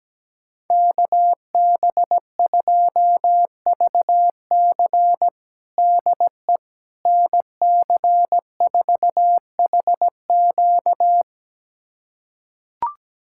Morse code, heard as a keyed tone, about 17 wpm, 700 Hz. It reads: KB2VC DE NC4HQ
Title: USA Callsign DE Callsign - 5 Characters - 17wpm